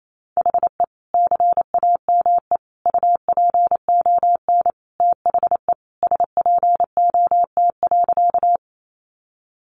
Morse code, heard as a keyed tone, about 28 wpm, 700 Hz. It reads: HE CAME UPON THE SPOT.
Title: Sentences from Top 1100 Words 28wpm